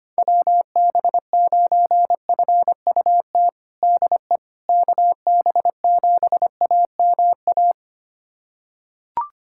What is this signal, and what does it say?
Morse code, heard as a keyed tone, about 25 wpm, 700 Hz. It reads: WB9FUT DE KB7AMA
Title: USA Callsign DE Callsign - 6 Characters - 25wpm